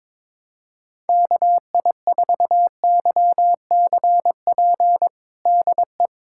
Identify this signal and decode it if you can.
Morse code, heard as a keyed tone, about 21 wpm, 700 Hz. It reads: KI4YCP DE
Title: USA Callsign DE Callsign - 6 Characters - 22wpm